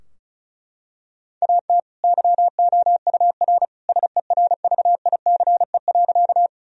Morse code, heard as a keyed tone, about 36 wpm, 700 Hz. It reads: AT YOUR SERVICE.
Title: Sentences from Top 1000 Words - Review Part 4 - 35wpm